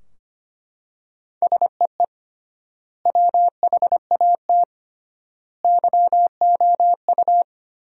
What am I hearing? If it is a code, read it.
Morse code, heard as a keyed tone, about 25 wpm, 700 Hz. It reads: SEE WHAT YOU
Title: 3x Word-Spacing - Sentences from Top 500 Words - 25wpm